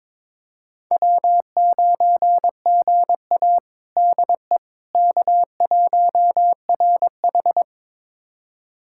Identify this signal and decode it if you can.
Morse code, heard as a keyed tone, about 23 wpm, 700 Hz. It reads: W9GA DE K1RH
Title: USA Callsign DE Callsign - 4 Characters - 22wpm